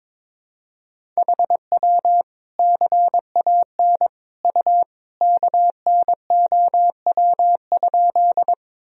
Morse code, heard as a keyed tone, about 22 wpm, 700 Hz. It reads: HW CAN U KNOW?
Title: Sentences from Top 100 Words and Abbreviations 22wpm